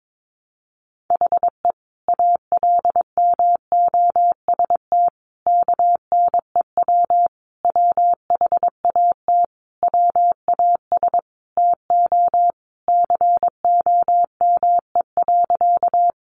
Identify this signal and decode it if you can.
Morse code, heard as a keyed tone, about 22 wpm, 700 Hz. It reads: HE ALMOST KNEW WHAT WAS TO COME.
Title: Sentences from Top 700 Words 22wpm